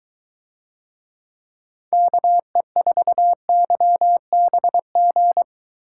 Morse code, heard as a keyed tone, about 22 wpm, 700 Hz. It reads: KE4YBG